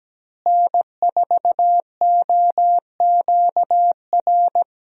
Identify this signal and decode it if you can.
Morse code, heard as a keyed tone, about 17 wpm, 700 Hz. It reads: N4OQR
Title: USA Call Signs - 5 Characters - Repeated 2x - 17wpm